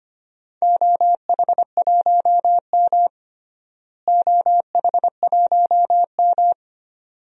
Morse code, heard as a keyed tone, about 25 wpm, 700 Hz. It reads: OH1M OH1M
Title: DXCC-100 Call Signs - 4 Characters - Repeated 2x - 25wpm